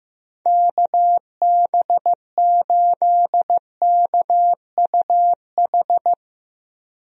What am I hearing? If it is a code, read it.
Morse code, heard as a keyed tone, about 16 wpm, 700 Hz. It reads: KB8KUH
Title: USA Call Signs - 6 Characters - Repeated 2x - 15wpm